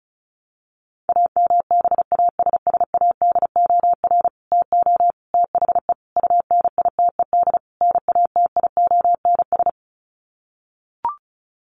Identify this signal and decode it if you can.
Morse code, heard as a keyed tone, about 34 wpm, 700 Hz. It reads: AMBASSADOR TO THE UNITED NATIONS
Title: Sets of 5 Words Encore2 35wpm